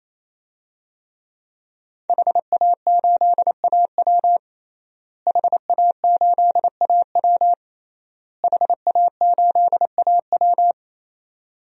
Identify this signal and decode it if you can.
Morse code, heard as a keyed tone, about 28 wpm, 700 Hz. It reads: HA8AW HA8AW HA8AW
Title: DXCC-100 Call Signs - 5 Characters - Repeated 3x - 28wpm